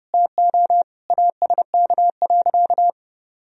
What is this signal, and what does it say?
Morse code, heard as a keyed tone, about 30 wpm, 700 Hz. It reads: TO ASK.
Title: Sentences from Top 1000 Words - Review Part 1 - 30wpm